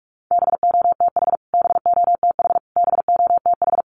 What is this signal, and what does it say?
Morse code, heard as a keyed tone, about 44 wpm, 700 Hz. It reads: BOTH BOTH BOTH
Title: Top 500 Words Repeated 45wpm